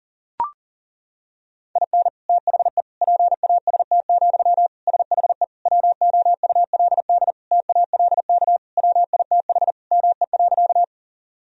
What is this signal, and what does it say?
Morse code, heard as a keyed tone, about 38 wpm, 700 Hz. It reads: IN THE PAST, SHE WOULD TALK WITH ME.
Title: Sentences from Top 1000 Words - Review Part 3 - 40wpm